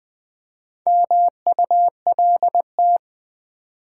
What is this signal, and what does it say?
Morse code, heard as a keyed tone, about 20 wpm, 700 Hz. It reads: MULT
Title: Build A Word - 4 Letters At A Time - 8-Letter Words - 20wpm